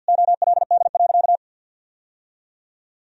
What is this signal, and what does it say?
Morse code, heard as a keyed tone, about 50 wpm, 700 Hz. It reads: OLD.
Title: Sentences from Top 1000 Words - Review Part 3 - 50wpm